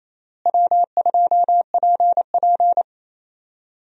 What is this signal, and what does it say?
Morse code, heard as a keyed tone, about 28 wpm, 700 Hz. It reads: W2PP
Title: USA Call Signs - 4 Characters - Repeated 3x - 28wpm